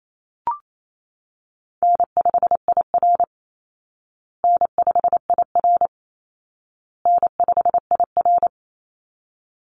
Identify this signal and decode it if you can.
Morse code, heard as a keyed tone, about 28 wpm, 700 Hz. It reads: N5IR N5IR N5IR
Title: Call Signs Repeated 2x - 28wpm